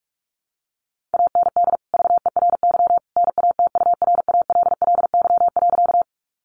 Morse code, heard as a keyed tone, about 44 wpm, 700 Hz. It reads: AND VERY NATURALLY.
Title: Sentences from Top 1300 Words 45wpm